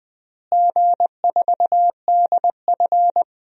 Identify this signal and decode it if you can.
Morse code, heard as a keyed tone, about 19 wpm, 690 Hz. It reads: G4DF
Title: DXCC-100 Call Signs - 4 Characters - Repeated 3x - 20wpm